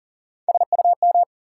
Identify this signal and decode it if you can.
Morse code, heard as a keyed tone, about 40 wpm, 700 Hz. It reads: SUM
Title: Sentences from Top 1000 Words - Review Part 6 - New Word Drill - 40wpm